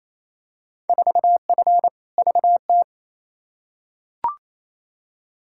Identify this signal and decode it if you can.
Morse code, heard as a keyed tone, about 28 wpm, 700 Hz. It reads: 4F VT
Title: Field Day Exchanges Encore - 28wpm